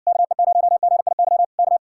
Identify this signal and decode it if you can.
Morse code, heard as a keyed tone, about 67 wpm, 700 Hz. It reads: KE0GIY K